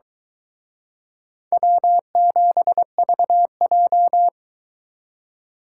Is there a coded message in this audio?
Yes: Morse code from 1.5 to 4.3 s, heard as a keyed tone, about 24 wpm, 700 Hz. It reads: W7VJ